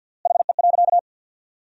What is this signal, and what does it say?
Morse code, heard as a keyed tone, about 50 wpm, 700 Hz. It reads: HE.
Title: 3x Word-Spacing - Sentences from Top 100 Words - 50wpm